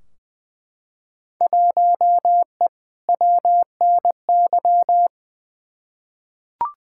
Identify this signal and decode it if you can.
Morse code, heard as a keyed tone, about 20 wpm, 700 Hz. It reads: 1E WNY
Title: Field Day Exchanges - 20wpm